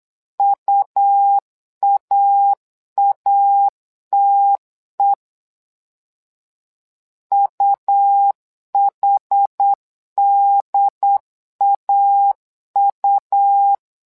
Morse code, heard as a keyed tone, about 8 wpm, 800 Hz. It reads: UAATE UHDAU
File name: LEZIONI / 2 UHDN / 2 UHD 8wpm 2258